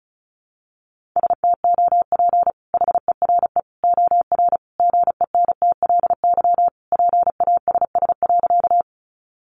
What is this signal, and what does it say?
Morse code, heard as a keyed tone, about 34 wpm, 700 Hz. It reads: STOP HERE OR GENTLY PASS.
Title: Sentences from Top 1500 Words 35wpm